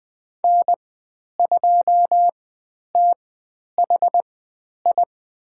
Morse code, heard as a keyed tone, about 20 wpm, spaced as for 10 wpm, 700 Hz. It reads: N2THI
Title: 109 Lesson 18g - Letter m - Call Signs 20wpm